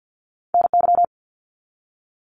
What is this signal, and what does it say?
Morse code, heard as a keyed tone, about 48 wpm, 700 Hz. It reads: NY
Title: US States Repeated 3x 50wpm